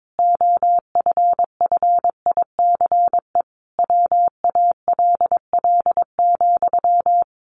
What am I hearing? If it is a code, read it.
Morse code, heard as a keyed tone, about 22 wpm, 700 Hz. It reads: OFFICE WALL,
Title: A Sound Of Thunder 22wpm